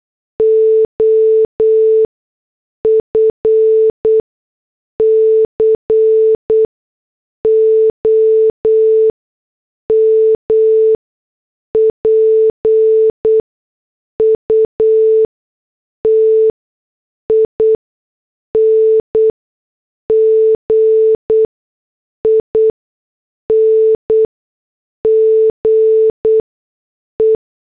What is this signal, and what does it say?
Morse code, heard as a keyed tone, about 8 wpm, spaced as for 6 wpm, 440 Hz. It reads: OF COMPUTING INGE
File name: morse / headlines